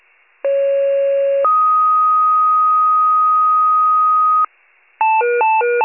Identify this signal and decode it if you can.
Two-tone page, heard as a one-second tone, then a three-second tone, about 550 then 1250 Hz. It is heard from 0.4 to 4.5 s.